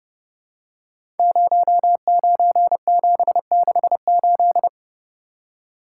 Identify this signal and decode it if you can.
Morse code, heard as a keyed tone, about 30 wpm, 700 Hz. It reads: 09768
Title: Code Groups - 5 Letter:Numbers - 30wpm